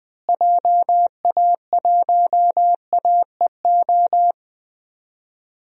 Morse code, heard as a keyed tone, about 20 wpm, 700 Hz. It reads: JA1AEO
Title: DXCC-100 Call Signs - 6 Characters - Repeated 2x - 20wpm